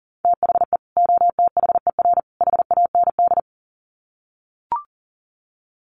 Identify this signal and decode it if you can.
Morse code, heard as a keyed tone, about 40 wpm, 700 Hz. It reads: THE OTHER HAND
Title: Sets of 3 Words 40wpm